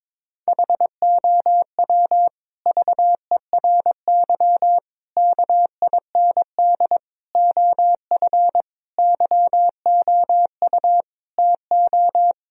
Morse code, heard as a keyed tone, about 22 wpm, 700 Hz. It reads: HOW VERY KIND OF YOU TO
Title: Sentences from Top 1000 Words - Review Part 1 - 22wpm